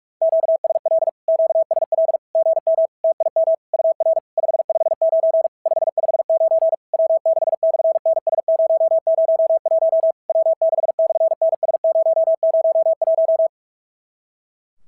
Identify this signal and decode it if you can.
Morse code, heard as a keyed tone, about 46 wpm, 650 Hz. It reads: QSL QSL GM TIM UR 559 559 W6/NS001 W6/NS001
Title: SOTA QSO Practice 45wpm